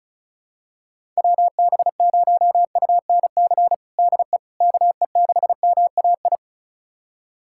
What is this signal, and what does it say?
Morse code, heard as a keyed tone, about 34 wpm, 700 Hz. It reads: WB0UNC DE KE6MAI
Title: USA Callsign DE Callsign 35wpm